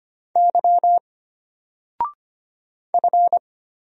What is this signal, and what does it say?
Morse code, heard as a keyed tone, about 25 wpm, 700 Hz. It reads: Y F
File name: Single-Letter-Number-25wpm